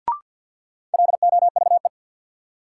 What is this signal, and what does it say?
Morse code, heard as a keyed tone, about 53 wpm, 700 Hz. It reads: LOVE
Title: Sentences from Top 400 Words - New Word Drill - 50wpm